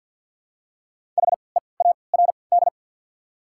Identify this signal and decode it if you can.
Morse code, heard as a keyed tone, about 54 wpm, spaced as for 29 wpm, 700 Hz. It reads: HEARD
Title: 3x Character Spacing - Top 500 Words - 50wpm